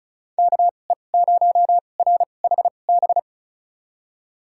Morse code, heard as a keyed tone, about 34 wpm, spaced as for 25 wpm, 700 Hz. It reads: KE0RHB
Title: USA Call Signs - 2x Character Spacing - 35wpm